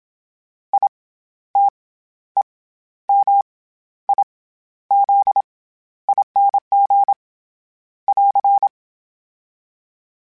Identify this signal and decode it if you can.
Morse code, heard as a keyed tone, about 27 wpm, 800 Hz. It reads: I T E M I Z ING <AR>
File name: Special INGs-mp3s / Special INGs-mp3s / SlowToFaster ing @25